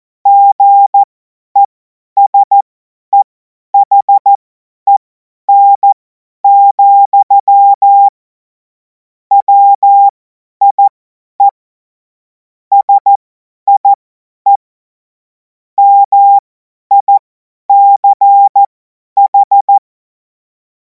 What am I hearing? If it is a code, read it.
Morse code, heard as a keyed tone, about 14 wpm, spaced as for 10 wpm, 800 Hz. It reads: GESEHEN, WIE SIE MICH
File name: Morsekurs / Witze / Witz 04. August 2026 50 F70